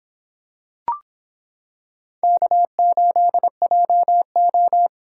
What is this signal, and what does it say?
Morse code, heard as a keyed tone, about 26 wpm, 700 Hz. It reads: K8JO